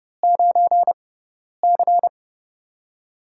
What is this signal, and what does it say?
Morse code, heard as a keyed tone, about 30 wpm, spaced as for 10 wpm, 700 Hz. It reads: 9C
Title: USA Call Signs - 4 Characters - 6x Character Spacing - 30wpm